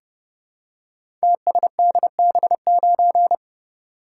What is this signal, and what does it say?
Morse code, heard as a keyed tone, about 30 wpm, 700 Hz. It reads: TSDB9
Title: Code Groups - 5 Letter:Numbers - 30wpm